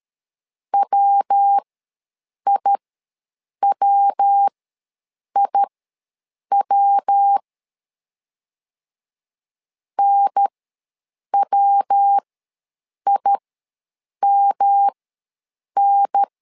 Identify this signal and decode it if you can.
Morse code, heard as a keyed tone, about 13 wpm, spaced as for 7 wpm, 800 Hz. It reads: WIWIW NWIMN
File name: LEZIONI / 3 MWRO / 3 MW ws 12wpm 1816